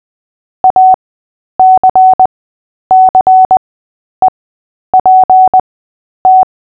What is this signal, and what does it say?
Morse code, heard as a keyed tone, about 20 wpm, spaced as for 10 wpm, 750 Hz. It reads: ACCEPT
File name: Short-Story-7-10wpm